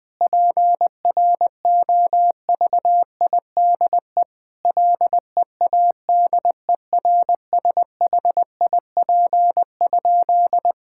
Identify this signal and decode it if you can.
Morse code, heard as a keyed tone, about 21 wpm, 700 Hz. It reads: PROVIDE LEADERSHIP?
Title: Sentences from Top 1600 Words 20wpm